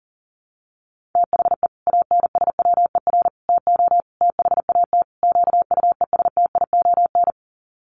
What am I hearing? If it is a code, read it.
Morse code, heard as a keyed tone, about 40 wpm, 700 Hz. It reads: THE ANSWER TO THAT QUESTION
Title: Sets of 5 Words Encore2 40wpm